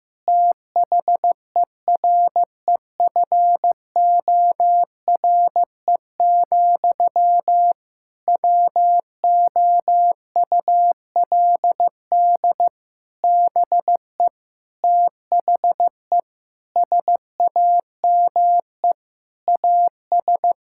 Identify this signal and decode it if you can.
Morse code, heard as a keyed tone, about 15 wpm, 700 Hz. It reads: THEREFORE, WOULD BE THE SAME AS
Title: Sentences from Top 1000 Words - Review Part 5 - 15wpm